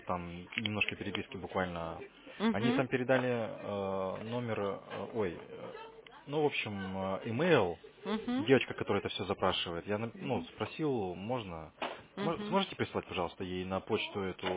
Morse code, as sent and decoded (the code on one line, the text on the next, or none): none